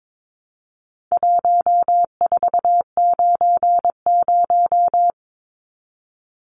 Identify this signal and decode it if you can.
Morse code, heard as a keyed tone, about 22 wpm, 700 Hz. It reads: 1490
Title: Numbers - 4-Digit 22wpm